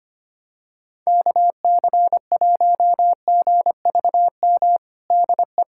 Morse code, heard as a keyed tone, about 25 wpm, 700 Hz. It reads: KC1GVM DE
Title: USA Callsign DE Callsign 25wpm